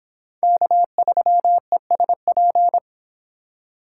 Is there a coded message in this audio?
Yes: Morse code from 0.4 to 2.8 s, heard as a keyed tone, about 23 wpm, 700 Hz. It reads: K3ESP